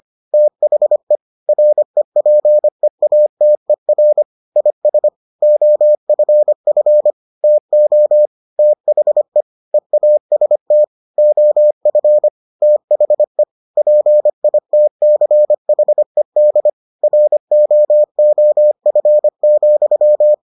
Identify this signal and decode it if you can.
Morse code, heard as a keyed tone, about 25 wpm, 600 Hz. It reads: THE REPEATER IS OFF TO THE EAST OF THE PITCHED ROOF,